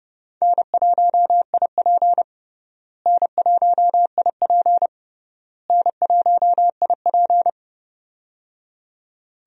Morse code, heard as a keyed tone, about 32 wpm, 700 Hz. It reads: N1IP N1IP N1IP
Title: USA Call Signs - 4 Characters - Repeated 3x - 30wpm